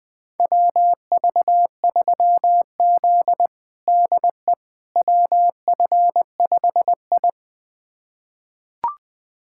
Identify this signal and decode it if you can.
Morse code, heard as a keyed tone, about 20 wpm, 700 Hz. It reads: WV3Z DE WF5I
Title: USA Callsign DE Callsign - 4 Characters - 20wpm